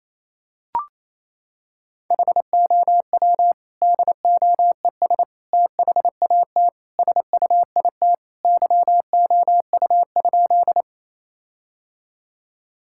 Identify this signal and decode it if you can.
Morse code, heard as a keyed tone, about 28 wpm, 700 Hz. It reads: HOW DOES THAT SUIT YOU?
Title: Sentences from Top 1600 Words 28wpm